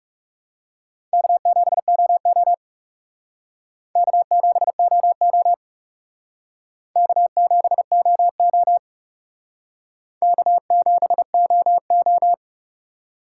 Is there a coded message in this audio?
Yes: Morse code from 1.1 to 12.3 s, heard as a keyed tone, about 38 wpm, 700 Hz. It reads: K7OO K7OO K7OO K7OO